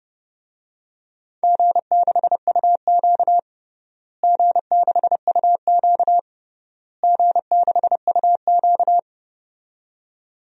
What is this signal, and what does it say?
Morse code, heard as a keyed tone, about 30 wpm, 700 Hz. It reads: G6UQ G6UQ G6UQ
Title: DXCC-100 Call Signs - 4 Characters - Repeated 3x - 30wpm